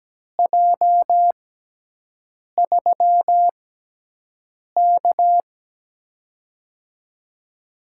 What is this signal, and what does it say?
Morse code, heard as a keyed tone, about 17 wpm, spaced as for 6 wpm, 700 Hz. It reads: J3K
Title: USA Call Signs - 6x Character Spacing - 17wpm